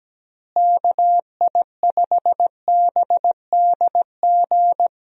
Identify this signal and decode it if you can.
Morse code, heard as a keyed tone, about 17 wpm, 700 Hz. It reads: KI5BDG
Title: USA Call Signs - 6 Characters - Repeated 2x - 17wpm